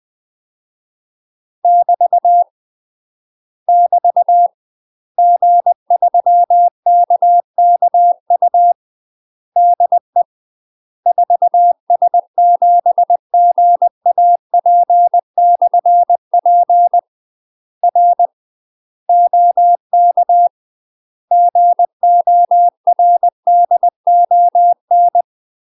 Morse code, heard as a keyed tone, about 20 wpm, 700 Hz. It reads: <BT> <BT> G3KKU DE 4S7GAP/P R OK GORDON
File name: UK MASTER QSOs TRAINING / 20WPM / UK MASTER QSOs 0013